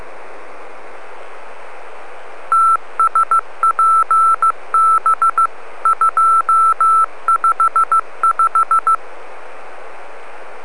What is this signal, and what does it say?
Morse code, heard as a keyed tone, about 15 wpm, 1330 Hz. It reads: TSPB 255